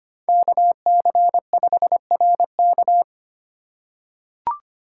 Morse code, heard as a keyed tone, about 25 wpm, 700 Hz. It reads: KC5RK
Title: USA Call Signs - 5 Characters - 4x Character Spacing - 25wpm